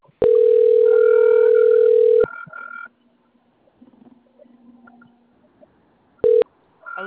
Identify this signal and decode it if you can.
Ringback tone (North America), 440 + 480 Hz.